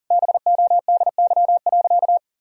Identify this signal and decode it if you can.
Morse code, heard as a keyed tone, about 40 wpm, 700 Hz. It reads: BODY.
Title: Sentences from Top 1000 Words - Review Part 2 - 40wpm